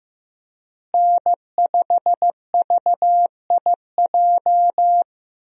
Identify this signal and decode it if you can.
Morse code, heard as a keyed tone, about 15 wpm, 700 Hz. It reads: N5VIJ
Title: USA Call Signs - 5 Characters - Repeated 2x - 15wpm